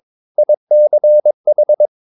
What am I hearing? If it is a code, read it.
Morse code, heard as a keyed tone, about 22 wpm, 600 Hz. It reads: ICH